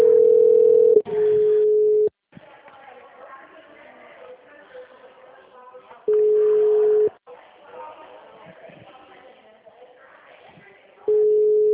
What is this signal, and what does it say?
Ringback tone (Europe, 425 Hz), with speech.
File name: out- -2024- - - .27488